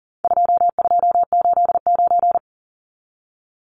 Morse code, heard as a keyed tone, about 40 wpm, 700 Hz. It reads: 2289
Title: Numbers - 4-Digit 40wpm